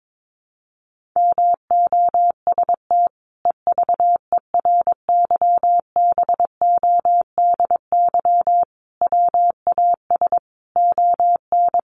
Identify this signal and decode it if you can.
Morse code, heard as a keyed tone, about 22 wpm, 700 Hz. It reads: MOST EVERYBODY WAS ON
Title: Sentences from Top 1400 Words 22wpm